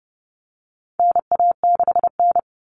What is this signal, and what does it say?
Morse code, heard as a keyed tone, about 30 wpm, 700 Hz. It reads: NA6N
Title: Common Cut Numbers - 4-Digit 30wpm